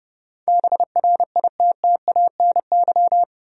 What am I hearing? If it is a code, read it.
Morse code, heard as a keyed tone, about 30 wpm, 700 Hz. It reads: BRITTANY
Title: USA Common Names - 30wpm